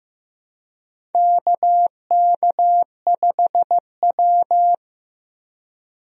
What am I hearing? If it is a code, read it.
Morse code, heard as a keyed tone, about 15 wpm, 700 Hz. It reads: KK5W
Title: USA Call Signs - 4 Characters - Repeated 2x - 15wpm